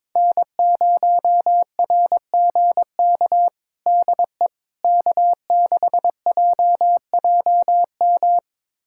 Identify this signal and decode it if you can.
Morse code, heard as a keyed tone, about 22 wpm, 700 Hz. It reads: N0RGK DE K6JJM
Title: USA Callsign DE Callsign - 5 Characters - 22wpm